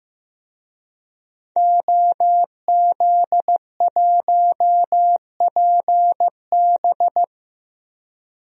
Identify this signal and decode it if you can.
Morse code, heard as a keyed tone, about 15 wpm, 700 Hz. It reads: OZ1PB
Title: DXCC-100 Call Signs - 5 Characters - 15wpm